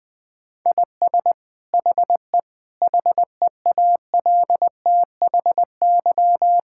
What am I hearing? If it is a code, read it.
Morse code, heard as a keyed tone, about 21 wpm, 700 Hz. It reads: IS HE HEALTHY
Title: Sentences from Top 1000 Words - Review Part 5 - 20wpm